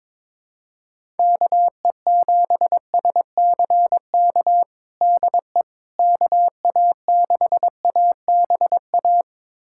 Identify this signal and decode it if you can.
Morse code, heard as a keyed tone, about 22 wpm, 700 Hz. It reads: KE7SCK DE KA6ABA
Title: USA Callsign DE Callsign - 6 Characters - 22wpm